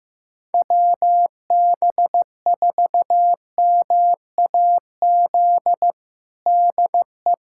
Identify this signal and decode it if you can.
Morse code, heard as a keyed tone, about 16 wpm, 700 Hz. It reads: WB4MAZ DE
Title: USA Callsign DE Callsign - 6 Characters - 15wpm